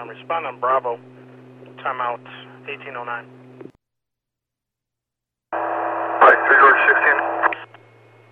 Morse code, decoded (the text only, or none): none